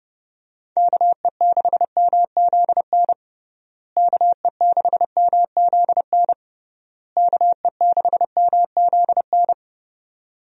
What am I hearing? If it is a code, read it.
Morse code, heard as a keyed tone, about 30 wpm, 700 Hz. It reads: KE6MZN KE6MZN KE6MZN
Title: USA Call Signs - 6 Characters - Repeated 3x - 30wpm